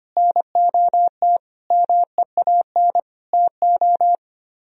Morse code, heard as a keyed tone, about 29 wpm, 700 Hz. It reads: NOT MEAN TO
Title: Speed-Racing - Sets of 5 Words 25wpm